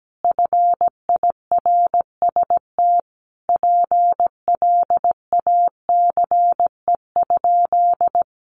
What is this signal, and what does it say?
Morse code, heard as a keyed tone, about 17 wpm, 700 Hz. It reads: FIRST PLACE?
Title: Sentences from Top 200 Words 17wpm